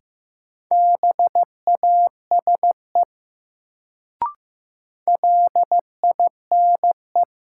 Morse code, heard as a keyed tone, about 15 wpm, 700 Hz. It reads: BASE LINE
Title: Sentences from Top 1000 Words - Review Part 1 - New Word Drill - 15wpm